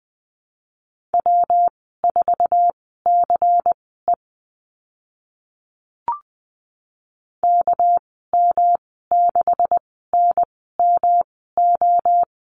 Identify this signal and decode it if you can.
Morse code, heard as a keyed tone, about 20 wpm, spaced as for 14 wpm, 700 Hz. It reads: W4CE KM6NMO
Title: 200 Lesson 32g - Letter k - Call Signs 20wpm